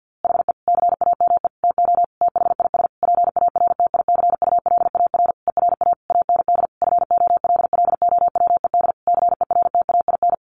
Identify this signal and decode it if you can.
Morse code, heard as a keyed tone, about 50 wpm, 700 Hz. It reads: HE CAME TO THIS PARTICULAR ERA AND FOLLOWED CERTAIN
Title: A Sound Of Thunder 50wpm